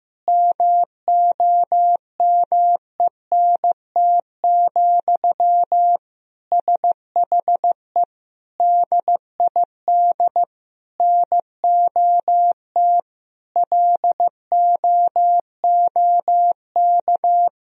Morse code, heard as a keyed tone, about 15 wpm, 700 Hz. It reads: MOMENT, SHE DID NOT LOOK
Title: Sentences from Top 1000 Words - Review Part 2 - 15wpm